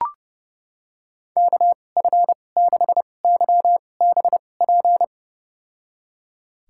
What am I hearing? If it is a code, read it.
Morse code, heard as a keyed tone, about 29 wpm, spaced as for 21 wpm, 700 Hz. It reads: KF6YBP
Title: USA Call Signs - 2x Character Spacing - 30wpm